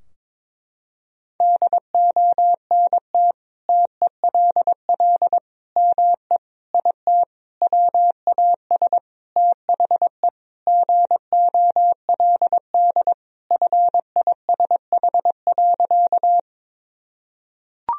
Morse code, heard as a keyed tone, about 22 wpm, 700 Hz. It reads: DONT TELL ME IT WAS THE GOLD FISH.
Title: Sentences from Top 1000 Words - Review Part 3 - 22wpm